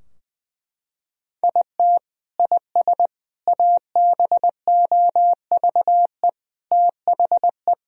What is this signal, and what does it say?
Morse code, heard as a keyed tone, about 21 wpm, 700 Hz. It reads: IT IS ABOVE THE
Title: Sentences from Top 1000 Words - Review Part 2 - 20wpm